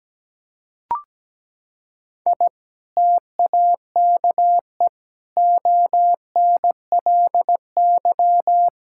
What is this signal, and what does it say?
Morse code, heard as a keyed tone, about 17 wpm, 700 Hz. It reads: I TAKE ONLY
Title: Sentences from Top 1000 Words - Review Part 5 - 17wpm